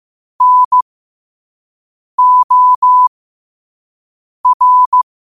Morse code, heard as a keyed tone, about 15 wpm, spaced as for 5 wpm, 1000 Hz. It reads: NOR
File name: qotdfiles / 5wpm